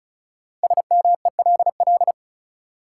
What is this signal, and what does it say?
Morse code, heard as a keyed tone, about 36 wpm, 700 Hz. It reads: SMELL
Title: Five Letter Words - 35wpm